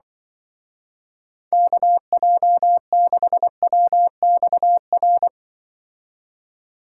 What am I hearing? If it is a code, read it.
Morse code, heard as a keyed tone, about 24 wpm, 700 Hz. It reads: KJ6WXR
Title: Speed Racing - USA 2x3 Call Signs 20wpm